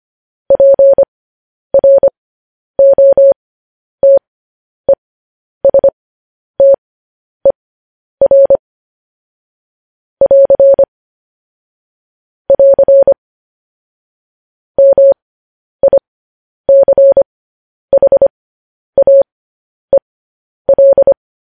Morse code, heard as a keyed tone, about 25 wpm, spaced as for 10 wpm, 550 Hz. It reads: PROTESTER <AR> <AR> MICHAEL